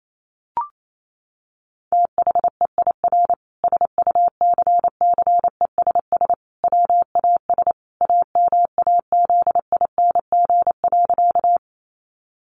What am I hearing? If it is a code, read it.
Morse code, heard as a keyed tone, about 28 wpm, 700 Hz. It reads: THEIR SUCCESS WAS AMAZING.
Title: Sentences from Top 1500 Words 28wpm